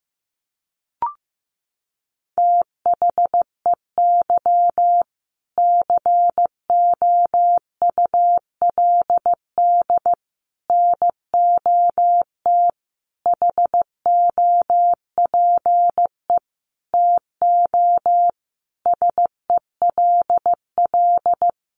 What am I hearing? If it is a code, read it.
Morse code, heard as a keyed tone, about 15 wpm, 700 Hz. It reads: THEY COULD NOT HOPE TO SELL
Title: Sentences from Top 1100 Words 15wpm